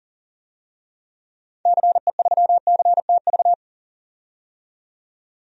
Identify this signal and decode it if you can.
Morse code, heard as a keyed tone, about 40 wpm, 700 Hz. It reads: CE3CTV
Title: DXCC-100 Call Signs - 6 Characters - 40wpm